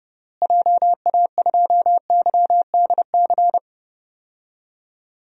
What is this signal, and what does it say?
Morse code, heard as a keyed tone, about 30 wpm, 700 Hz. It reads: JA2YDC
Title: DXCC-100 Call Signs - 6 Characters - Repeated 3x - 30wpm